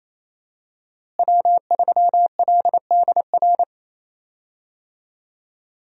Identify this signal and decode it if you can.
Morse code, heard as a keyed tone, about 27 wpm, 700 Hz. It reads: W3LDR
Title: USA Call Signs - 5 Characters - Repeated 3x - 28wpm